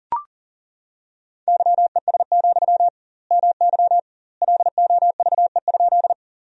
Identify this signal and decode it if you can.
Morse code, heard as a keyed tone, about 40 wpm, 700 Hz. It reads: YES, MY LOVE?
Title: 2x Word-Spacing - Sentences from Top 500 Words - 40wpm